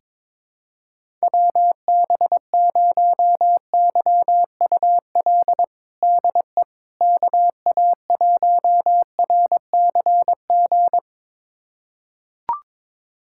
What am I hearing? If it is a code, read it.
Morse code, heard as a keyed tone, about 22 wpm, 700 Hz. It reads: WB0YUL DE KA1RCG
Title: USA Callsign DE Callsign 22wpm